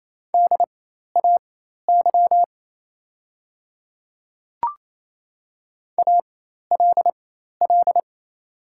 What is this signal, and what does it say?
Morse code, heard as a keyed tone, about 28 wpm, spaced as for 13 wpm, 700 Hz. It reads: DAY ALL
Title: 4x Character Spacing - Top 200 Words - 28wpm